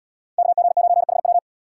Morse code, heard as a keyed tone, about 76 wpm, 700 Hz. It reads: KG9HW